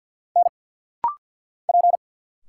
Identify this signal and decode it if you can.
Morse code, heard as a keyed tone, about 46 wpm, 700 Hz. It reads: N P
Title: Single Letters - Rapid-Fire 50wpm